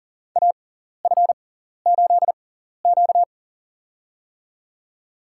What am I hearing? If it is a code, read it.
Morse code, heard as a keyed tone, about 40 wpm, spaced as for 14 wpm, 700 Hz. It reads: AF8Q
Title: USA Call Signs - 4 Characters - 6x Character Spacing - 40wpm